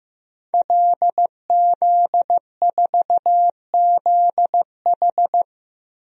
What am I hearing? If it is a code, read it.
Morse code, heard as a keyed tone, about 15 wpm, 700 Hz. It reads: LZ4ZH
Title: DXCC-100 Call Signs - 5 Characters - Repeated 3x - 15wpm